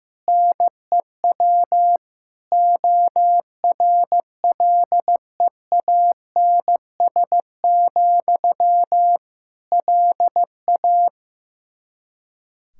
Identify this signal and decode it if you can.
Morse code, heard as a keyed tone, about 15 wpm, 700 Hz. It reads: NEW ORLEANS, LA
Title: Practice QSO - Exchange 1 - Beginning - 15wpm